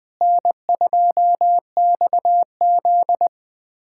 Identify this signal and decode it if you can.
Morse code, heard as a keyed tone, about 20 wpm, 700 Hz. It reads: N2XZ
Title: USA Callsign DE Callsign - 4 Characters - 20wpm